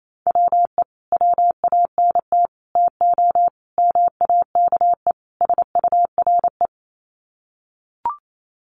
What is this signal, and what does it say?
Morse code, heard as a keyed tone, about 28 wpm, 700 Hz. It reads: WE WANT TO MAKE SURE
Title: Sets of 5 Words Encore2 28wpm